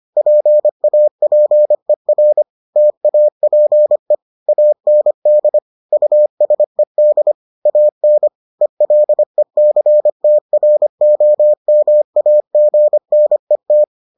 Morse code, heard as a keyed tone, about 25 wpm, 600 Hz. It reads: PAPER TAPE AND USED AN ELECTROMAGNET